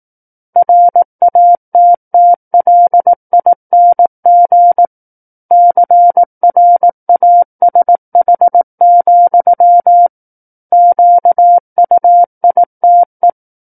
Morse code, heard as a keyed tone, about 18 wpm, 700 Hz. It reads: RATTLING CRASH, QUITE
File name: 18 WPM